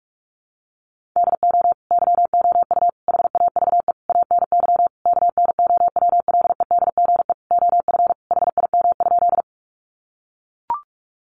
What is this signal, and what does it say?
Morse code, heard as a keyed tone, about 44 wpm, 700 Hz. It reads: DO YOU HAVE ANY KNOWLEDGE OF HIM?
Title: Sentences from Top 600 Words 45wpm